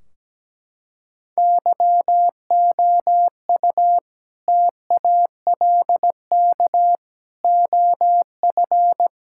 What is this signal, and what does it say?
Morse code, heard as a keyed tone, about 17 wpm, 700 Hz. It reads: YOU TALK OF
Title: Sentences from Top 1600 Words 17wpm